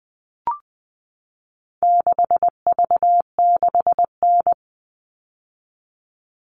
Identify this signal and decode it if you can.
Morse code, heard as a keyed tone, about 20 wpm, 700 Hz. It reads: 6V6N
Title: All Cut Numbers - 4-Digit 20wpm